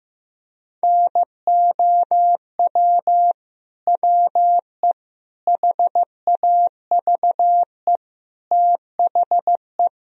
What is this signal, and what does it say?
Morse code, heard as a keyed tone, about 15 wpm, 700 Hz. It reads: NOW WE HAVE THE
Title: Sentences from Top 1000 Words - Review Part 3 - 15wpm